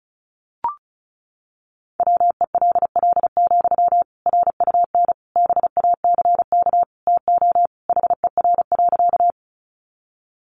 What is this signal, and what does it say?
Morse code, heard as a keyed tone, about 34 wpm, 700 Hz. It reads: WELL, RUN BACK TO HER.
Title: Sentences from Top 300 Words 35wpm